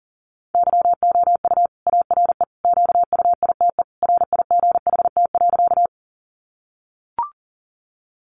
Morse code, heard as a keyed tone, about 40 wpm, 700 Hz. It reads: YOU ARE QUITE RIGHT.
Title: Sentences from Top 700 Words 40wpm